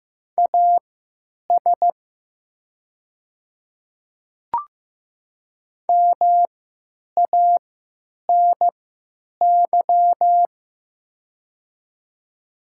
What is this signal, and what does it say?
Morse code, heard as a keyed tone, about 15 wpm, spaced as for 9 wpm, 700 Hz. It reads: AS MANY
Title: 3x Character Spacing - Top 100 Words - 15wpm